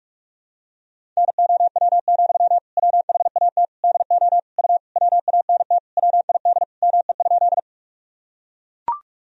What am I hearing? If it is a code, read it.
Morse code, heard as a keyed tone, about 42 wpm, 700 Hz. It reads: NOW, WHAT DO U WANT WID ME?
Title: Sentences from Top 200 Words and Abbreviations 45wpm